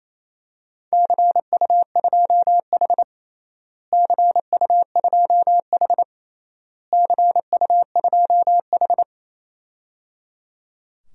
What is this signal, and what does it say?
Morse code, heard as a keyed tone, about 27 wpm, 700 Hz. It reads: CU2H CU2H CU2H
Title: DXCC-100 Call Signs - 4 Characters - Repeated 3x - 28wpm